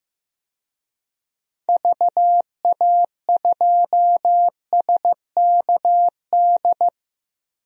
Morse code, heard as a keyed tone, about 15 wpm, 700 Hz. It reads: VA2SKD
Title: DXCC-100 Call Signs - 6 Characters - Repeated 2x - 15wpm